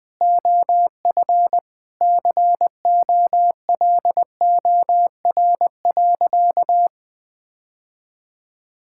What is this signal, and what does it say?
Morse code, heard as a keyed tone, about 21 wpm, 700 Hz. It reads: OF COLOR.
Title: Sentences with 6-Word Phrases - 20wpm